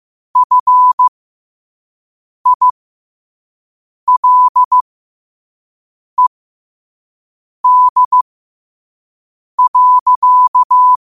Morse code, heard as a keyed tone, about 15 wpm, spaced as for 5 wpm, 1000 Hz. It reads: FILED.
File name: 5wpm